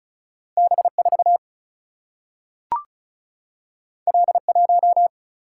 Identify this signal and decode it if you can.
Morse code, heard as a keyed tone, about 36 wpm, 700 Hz. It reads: B4 L1
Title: Single-Letter Single-Number 35wpm